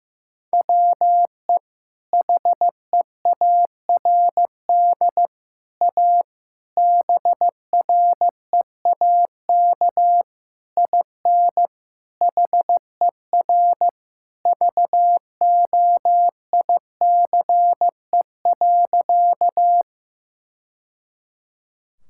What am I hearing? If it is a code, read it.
Morse code, heard as a keyed tone, about 16 wpm, 700 Hz. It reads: WE HEARD A BREAK IN HER VOICE.
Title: Sentences from Top 1000 Words - Review Part 6 - 15wpm